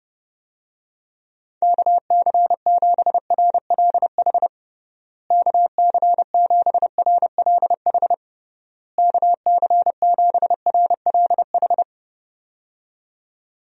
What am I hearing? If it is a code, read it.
Morse code, heard as a keyed tone, about 30 wpm, 700 Hz. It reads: KC7RLH KC7RLH KC7RLH
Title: USA Call Signs - 6 Characters - Repeated 3x - 30wpm